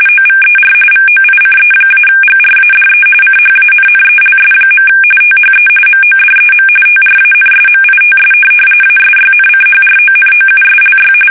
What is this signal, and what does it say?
RTTY, about 45.45 baud. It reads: OVER VOICE-GRADE CHANNELS. THE MODEM TRANSMITS ASYNCHRONOUSLY
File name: fsk850 / teletype-corp